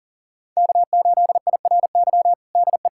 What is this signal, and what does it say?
Morse code, heard as a keyed tone, about 40 wpm, 700 Hz. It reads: K8IRY DE
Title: USA Callsign DE Callsign - 5 Characters - 40wpm